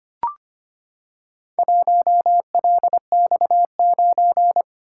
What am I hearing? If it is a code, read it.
Morse code, heard as a keyed tone, about 25 wpm, 700 Hz. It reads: 1LX9
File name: practice / 25wpm / Four Letter-Number Combinations - 25wpm